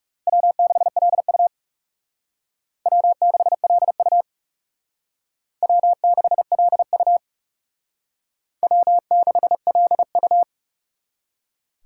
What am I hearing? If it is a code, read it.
Morse code, heard as a keyed tone, about 37 wpm, 700 Hz. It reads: W6LU W6LU W6LU W6LU
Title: Speed Racing - USA 1x2 Call Signs 30wpm